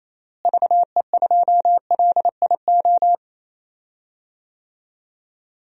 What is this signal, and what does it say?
Morse code, heard as a keyed tone, about 28 wpm, 700 Hz. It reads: VE2LIO
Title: DXCC-100 Call Signs - 6 Characters - Repeated 2x - 28wpm